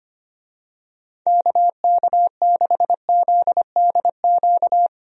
Morse code, heard as a keyed tone, about 25 wpm, 700 Hz. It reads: KK6ZDQ
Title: USA Call Signs - 6 Characters - 25wpm